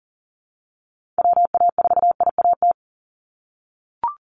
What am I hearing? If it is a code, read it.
Morse code, heard as a keyed tone, about 40 wpm, 700 Hz. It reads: WA4IAT
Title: USA Call Signs 40wpm